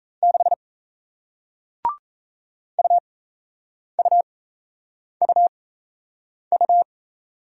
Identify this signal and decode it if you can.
Morse code, heard as a keyed tone, about 38 wpm, spaced as for 8 wpm, 700 Hz. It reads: 6 UUUU